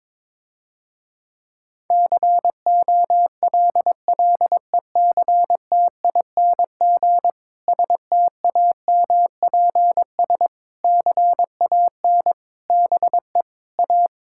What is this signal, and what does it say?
Morse code, heard as a keyed tone, about 21 wpm, 700 Hz. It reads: COLLECTING STAMPS CAN BE A
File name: Difficult Sentences with Hints - 22wpm